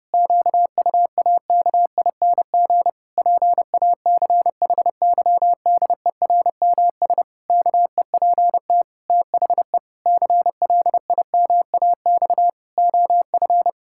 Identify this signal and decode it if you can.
Morse code, heard as a keyed tone, about 30 wpm, 700 Hz. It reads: QUAKING PACHYDERMS KEPT THE CLIMAX OF
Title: Pangram Challenge v2 - 30wpm